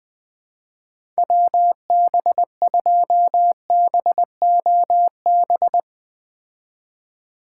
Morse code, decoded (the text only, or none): WB2BOB